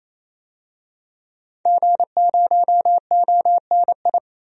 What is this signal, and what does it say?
Morse code, heard as a keyed tone, about 28 wpm, 700 Hz. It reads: G0ONI
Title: DXCC-100 Call Signs - 5 Characters - 28wpm